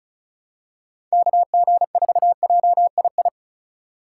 Morse code, heard as a keyed tone, about 36 wpm, 700 Hz. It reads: KG4JII
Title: USA Call Signs - 6 Characters - 3x Character Spacing - 35wpm